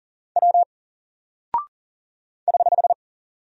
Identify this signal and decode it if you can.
Morse code, heard as a keyed tone, about 40 wpm, 700 Hz. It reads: W <HH>
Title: Single Character (Letter:Number:Punctuation:Prosign) Alt 40wpm